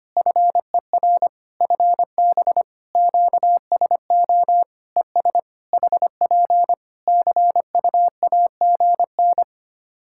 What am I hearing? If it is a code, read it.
Morse code, heard as a keyed tone, about 25 wpm, 700 Hz. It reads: FER FB QSO ES HP CUAGN
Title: K1USN SST QSO Practice 25wpm